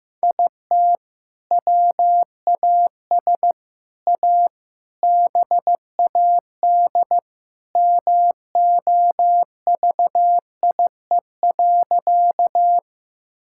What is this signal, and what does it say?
Morse code, heard as a keyed tone, about 16 wpm, 700 Hz. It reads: IT WAS A BAD MOVIE.
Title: Sentences from Top 1000 Words - Review Part 5 - 15wpm